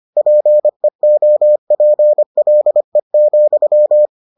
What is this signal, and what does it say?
Morse code, heard as a keyed tone, about 25 wpm, 600 Hz. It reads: PEOPLE,